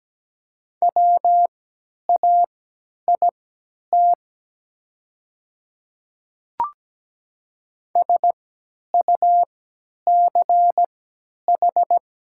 Morse code, heard as a keyed tone, about 17 wpm, spaced as for 10 wpm, 700 Hz. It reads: WAIT SUCH
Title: 3x Character Spacing - Top 500 Words - 17wpm